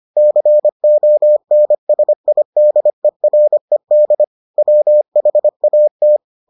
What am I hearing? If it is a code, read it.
Morse code, heard as a keyed tone, about 25 wpm, 600 Hz. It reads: CONSIDERED WHAT